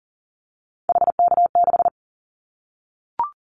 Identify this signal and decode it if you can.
Morse code, heard as a keyed tone, about 40 wpm, 700 Hz. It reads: HK6
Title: Callsign Prefixes for Top 100 DXCC - 40wpm